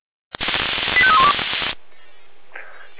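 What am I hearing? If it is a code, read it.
Five-tone selective call (ZVEI), heard as quick runs of five tones, tones of about 70 ms. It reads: 85312
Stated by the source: ZVEI five-tone selective call received with an SDR, recorded by the Feuerwehr Orlamünde fire brigade.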